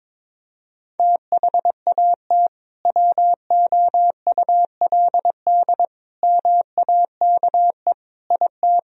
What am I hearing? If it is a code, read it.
Morse code, heard as a keyed tone, about 21 wpm, 700 Hz. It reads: THAT WOULD MAKE IT
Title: Sentences from Top 1000 Words - Review Part 2 - 22wpm